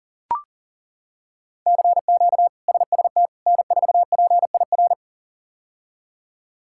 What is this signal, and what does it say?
Morse code, heard as a keyed tone, about 40 wpm, 700 Hz. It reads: CQ SST N4PIR
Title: K1USN SST QSO Practice - 40wpm